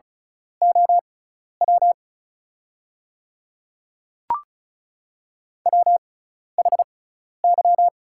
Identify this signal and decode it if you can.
Morse code, heard as a keyed tone, about 36 wpm, spaced as for 12 wpm, 690 Hz. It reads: OW WHY
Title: 6x Character Spacing - Top 300 Words - 35wpm